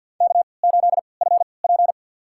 Morse code, heard as a keyed tone, about 52 wpm, spaced as for 29 wpm, 700 Hz. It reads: K8FP
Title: USA Call Signs - 4 Characters - 3x Character Spacing - 50wpm